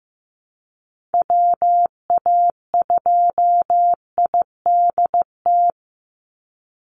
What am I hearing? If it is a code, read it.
Morse code, heard as a keyed tone, about 15 wpm, 700 Hz. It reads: WA2IDT
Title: USA Call Signs 15wpm